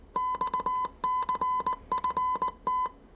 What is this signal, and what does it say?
Morse code, heard as a keyed tone, about 19 wpm, 1010 Hz. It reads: XCFT
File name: 业余无线电 / ●未分类 / 字码80